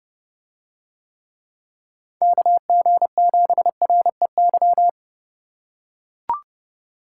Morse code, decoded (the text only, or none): KG7REY